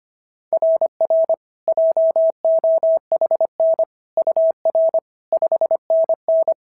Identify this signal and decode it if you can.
Morse code, heard as a keyed tone, about 25 wpm, 690 Hz. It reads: RR JOHN UR 5NN
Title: SOTA QSO Practice 25wpm